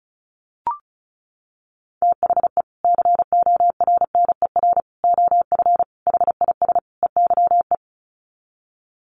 Morse code, heard as a keyed tone, about 34 wpm, 700 Hz. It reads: THE CORNER OF HIS EYE
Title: Sets of 5 Words Encore2 35wpm